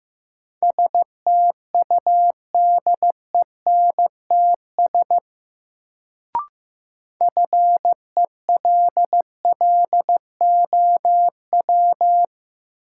Morse code, heard as a keyed tone, about 15 wpm, 700 Hz. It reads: STUDENTS FELLOW
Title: Sets of 5 Words Encore - New Word Drill - 15wpm